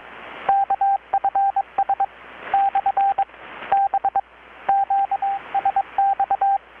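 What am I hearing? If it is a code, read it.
Morse code, heard as a keyed tone, about 22 wpm, 790 Hz, with radio noise. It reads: KFS / B QSX
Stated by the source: Morse from KPH's Night of Nights XX maritime radiotelegraphy event, received in Chiapas, Mexico about 3,800 km away on a Kenwood R-600 with a grounded 10 m long-wire antenna.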